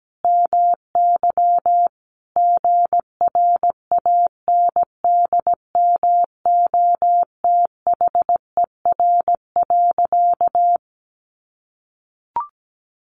Morse code, heard as a keyed tone, about 17 wpm, 700 Hz. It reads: MY GRANDMOTHER.
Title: Sentences from Top 1200 Words 17wpm